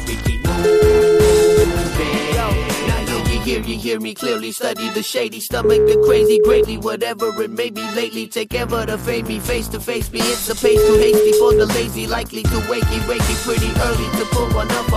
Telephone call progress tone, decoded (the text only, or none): ringback tone